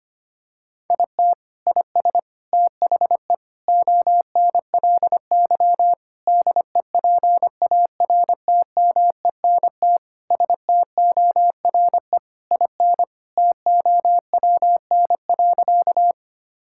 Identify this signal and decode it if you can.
Morse code, heard as a keyed tone, about 25 wpm, 700 Hz. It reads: IT IS THE ONLY DEPARTMENT STORE IN TOWN.
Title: Sentences from Top 1000 Words - Review Part 5 - 25wpm